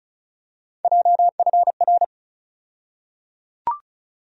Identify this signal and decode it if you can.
Morse code, heard as a keyed tone, about 36 wpm, 700 Hz. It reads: JFR
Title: USA Call Sign Postfixes - 3 Letters 35wpm